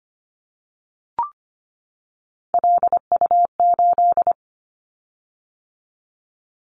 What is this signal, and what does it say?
Morse code, heard as a keyed tone, about 25 wpm, 700 Hz. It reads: LU8
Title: Callsign Prefixes for Top 100 DXCC - 25wpm